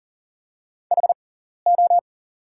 Morse code, heard as a keyed tone, about 40 wpm, spaced as for 14 wpm, 700 Hz. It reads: HO
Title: 6x Character Spacing - Top 100 Words - 40wpm